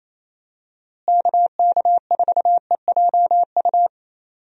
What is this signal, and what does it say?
Morse code, heard as a keyed tone, about 27 wpm, 700 Hz. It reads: KK4EJU
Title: USA Call Signs - 6 Characters - Repeated 3x - 28wpm